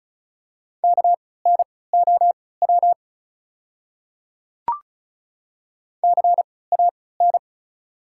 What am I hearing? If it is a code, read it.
Morse code, heard as a keyed tone, about 34 wpm, spaced as for 20 wpm, 690 Hz. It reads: KNOW CAN
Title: 3x Character Spacing - Top 100 Words - 35wpm